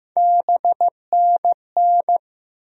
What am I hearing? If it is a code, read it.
Morse code, heard as a keyed tone, about 15 wpm, 700 Hz. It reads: BNN
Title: USA Call Sign Postfixes - 3 Letters 15wpm